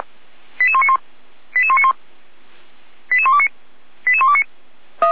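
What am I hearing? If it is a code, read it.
Five-tone selective call (ZVEI), heard as quick runs of five tones, tones of about 70 ms. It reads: 88191 88191 88129 88129
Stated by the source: ZVEI five-tone selective call received with an SDR.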